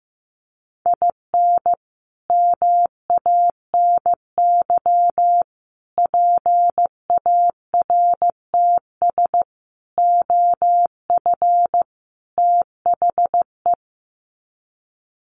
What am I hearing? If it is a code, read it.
Morse code, heard as a keyed tone, about 15 wpm, 700 Hz. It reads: IN MANY PARTS OF THE
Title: Sets of 5 Words Encore 15wpm